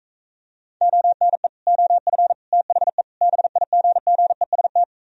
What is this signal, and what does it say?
Morse code, heard as a keyed tone, about 45 wpm, 700 Hz. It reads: ONE OF THE BIGGEST